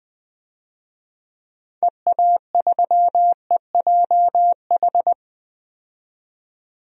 Morse code, heard as a keyed tone, about 20 wpm, 700 Hz. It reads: EA3EJH
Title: DXCC-100 Call Signs - 6 Characters - 20wpm